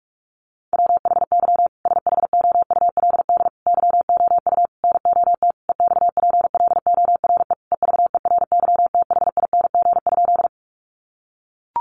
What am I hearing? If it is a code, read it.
Morse code, heard as a keyed tone, about 44 wpm, 700 Hz. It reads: WHY SHOULD YOU NOT EXPLORE EVERYTHING?
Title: Sentences from Top 1100 Words 45wpm